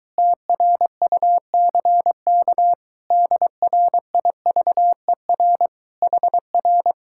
Morse code, heard as a keyed tone, about 22 wpm, 700 Hz. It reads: TRUCK DRIVER HR